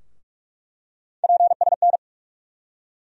Morse code, heard as a keyed tone, about 46 wpm, 700 Hz. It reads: PSN
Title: USA Call Sign Postfixes - 3 Letters 45wpm